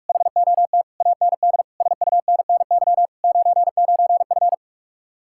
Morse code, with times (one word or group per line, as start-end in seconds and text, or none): 0.1-0.8 HOT
1.0-1.6 AND
1.8-3.1 SUNNY
3.2-4.5 99F